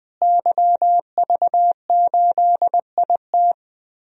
Morse code, heard as a keyed tone, about 21 wpm, 700 Hz. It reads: YV8IT
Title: DXCC-100 Call Signs - 5 Characters - Repeated 3x - 20wpm